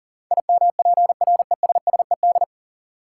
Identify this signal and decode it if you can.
Morse code, heard as a keyed tone, about 40 wpm, 700 Hz. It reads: IMPRESSED
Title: Sentences from Top 1600 Words - New Word Drill - 40wpm